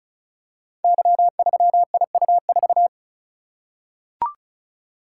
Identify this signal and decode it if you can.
Morse code, heard as a keyed tone, about 36 wpm, 700 Hz. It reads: Y3IU4
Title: Code Groups - 5 Letter:Numbers - 35wpm